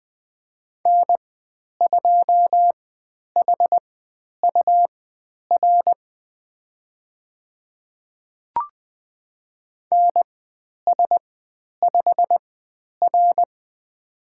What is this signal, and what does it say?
Morse code, heard as a keyed tone, about 20 wpm, spaced as for 10 wpm, 700 Hz. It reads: N2HUR NS5R
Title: 102 Lesson 17g - Letter u - Call Signs 20wpm